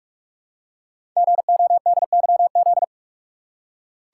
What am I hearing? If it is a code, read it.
Morse code, heard as a keyed tone, about 44 wpm, 690 Hz. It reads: GODYZ
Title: Code Groups - 5 Letters - 45wpm